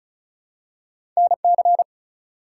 Morse code, heard as a keyed tone, about 36 wpm, 700 Hz. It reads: NC
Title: US States and Canadian Provinces 35wpm